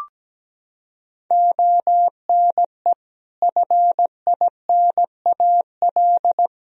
Morse code, heard as a keyed tone, about 17 wpm, 700 Hz. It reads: ONE FINAL
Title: Sentences from Top 1000 Words - Review Part 2 - 17wpm